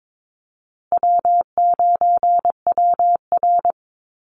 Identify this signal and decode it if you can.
Morse code, heard as a keyed tone, about 22 wpm, 700 Hz. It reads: W9WR
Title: Easy Call Signs Repeated 2x - 22wpm